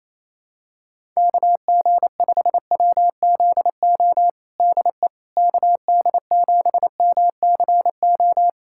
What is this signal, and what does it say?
Morse code, heard as a keyed tone, about 28 wpm, 700 Hz. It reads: KG5WZO DE KD7MCO
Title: USA Callsign DE Callsign 28wpm